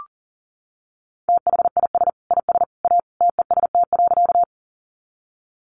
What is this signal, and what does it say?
Morse code, heard as a keyed tone, about 40 wpm, 700 Hz. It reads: THIS IS A TEST.
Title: Sentences from Top 500 Words 40wpm